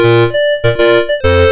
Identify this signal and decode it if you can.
There is no code.